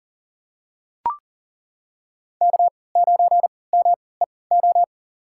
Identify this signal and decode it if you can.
Morse code, heard as a keyed tone, about 38 wpm, spaced as for 22 wpm, 700 Hz. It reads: K9MEO
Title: USA Call Signs - 3x Character Spacing - 40wpm